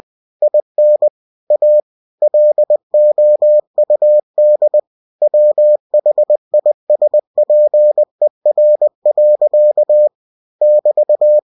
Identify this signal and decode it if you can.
Morse code, heard as a keyed tone, about 20 wpm, 600 Hz. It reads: IN A LOUD WHISPER. <BT>
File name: cw-books / winnie-the-pooh / winnie-20wpm-page-0007